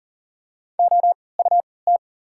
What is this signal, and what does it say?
Morse code, heard as a keyed tone, about 40 wpm, spaced as for 23 wpm, 700 Hz. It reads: OUT
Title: 3x Character Spacing - Top 500 Words - 40wpm